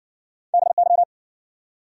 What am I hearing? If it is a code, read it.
Morse code, heard as a keyed tone, about 60 wpm, 700 Hz. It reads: BY